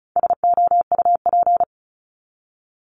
Morse code, heard as a keyed tone, about 34 wpm, 700 Hz. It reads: SOUP
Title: Top 751-1000 Words 35wpm